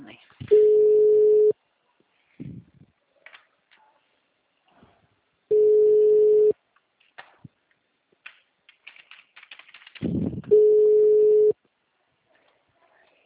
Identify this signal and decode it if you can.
Ringback tone (Europe, 425 Hz), with speech.